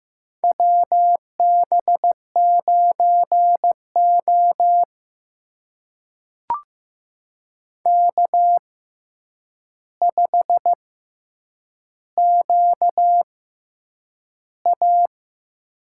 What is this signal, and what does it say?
Morse code, heard as a keyed tone, about 15 wpm, spaced as for 8 wpm, 700 Hz. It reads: WB9O K5QA
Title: USA Call Signs - 4 Characters - 6x Character Spacing - 15wpm